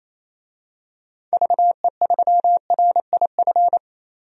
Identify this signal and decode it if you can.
Morse code, heard as a keyed tone, about 28 wpm, 700 Hz. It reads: VE3RIF
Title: DXCC-100 Call Signs - 6 Characters - Repeated 3x - 28wpm